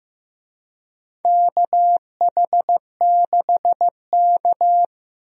Morse code, heard as a keyed tone, about 15 wpm, 700 Hz. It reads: KH6K
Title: DXCC-100 Call Signs - 4 Characters - Repeated 3x - 15wpm